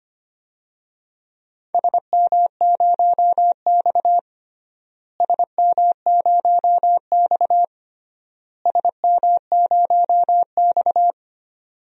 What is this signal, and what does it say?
Morse code, heard as a keyed tone, about 25 wpm, 700 Hz. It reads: SM0X SM0X SM0X
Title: DXCC-100 Call Signs - 4 Characters - Repeated 3x - 25wpm